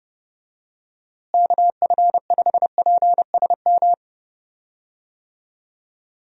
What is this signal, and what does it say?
Morse code, heard as a keyed tone, about 32 wpm, 700 Hz. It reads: KF5PSM
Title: USA Call Signs - 6 Characters - Repeated 2x - 30wpm